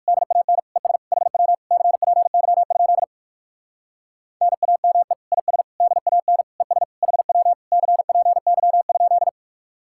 Morse code, heard as a keyed tone, about 58 wpm, 700 Hz. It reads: DAN ES HW CPY? NAME IS DAN ES HW CPY?